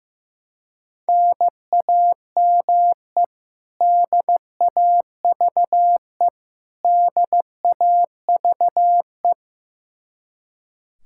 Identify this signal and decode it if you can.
Morse code, heard as a keyed tone, about 15 wpm, 700 Hz. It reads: NAME DAVE DAVE
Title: Practice QSO - Exchange 1 - Beginning - 15wpm